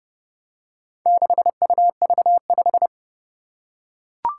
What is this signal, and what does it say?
Morse code, heard as a keyed tone, about 30 wpm, 700 Hz. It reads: 6UV5
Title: Four Challenging Letter-Number Combinations 30wpm